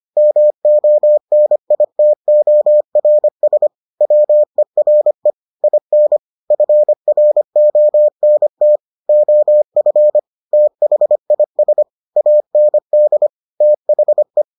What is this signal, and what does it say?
Morse code, heard as a keyed tone, about 25 wpm, 600 Hz. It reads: MONITORS WERE IN FRONT OF THIS AND THE